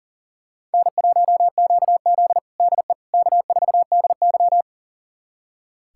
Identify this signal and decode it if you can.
Morse code, heard as a keyed tone, about 40 wpm, 700 Hz. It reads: N1QZ DE K4DY
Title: USA Callsign DE Callsign - 4 Characters - 40wpm